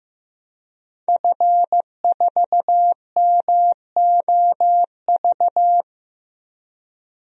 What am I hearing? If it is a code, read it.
Morse code, heard as a keyed tone, about 15 wpm, 700 Hz. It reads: F4MOV
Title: DXCC-100 Call Signs - 5 Characters - 15wpm